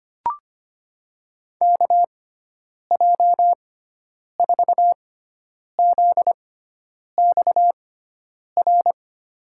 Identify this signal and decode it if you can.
Morse code, heard as a keyed tone, about 25 wpm, spaced as for 9 wpm, 700 Hz. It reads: KJ4ZXR
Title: USA Call Signs - 6 Characters - 6x Character Spacing - 25wpm